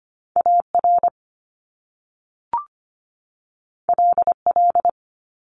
Morse code, heard as a keyed tone, about 25 wpm, 700 Hz. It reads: AR LL
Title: Top Two Letter Combinations in Words 25wpm